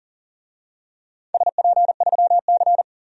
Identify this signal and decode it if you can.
Morse code, heard as a keyed tone, about 39 wpm, 700 Hz. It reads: SP3C
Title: DXCC-100 Call Signs - 4 Characters - 40wpm